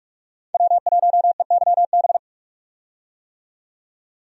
Morse code, heard as a keyed tone, about 42 wpm, 700 Hz. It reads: W1EYB
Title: USA Call Signs - 5 Characters - 45wpm